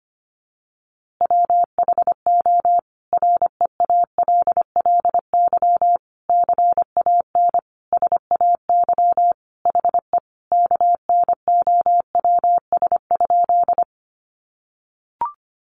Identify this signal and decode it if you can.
Morse code, heard as a keyed tone, about 25 wpm, 700 Hz. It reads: WHO REALLY CAN SAY HE KNOWS?
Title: A Sound Of Thunder 25wpm